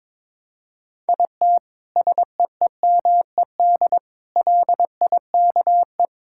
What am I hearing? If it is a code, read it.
Morse code, heard as a keyed tone, about 22 wpm, 700 Hz. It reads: IT SEEMED LIKE
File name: Sets of 9 Words - 22wpm